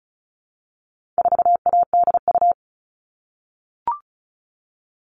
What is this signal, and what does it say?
Morse code, heard as a keyed tone, about 34 wpm, 700 Hz. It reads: 4ADU
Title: All Cut Numbers - 4-Digit 35wpm